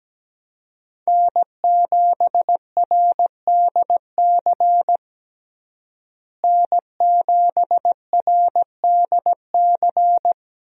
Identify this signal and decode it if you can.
Morse code, heard as a keyed tone, about 17 wpm, 700 Hz. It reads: N7RDC N7RDC
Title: USA Call Signs - 5 Characters - Repeated 2x - 17wpm